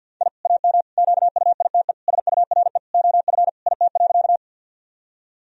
Morse code, heard as a keyed tone, about 50 wpm, 700 Hz. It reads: I AM QUITE SURE OF IT.
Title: Sentences from Top 1000 Words - Review Part 2 - 50wpm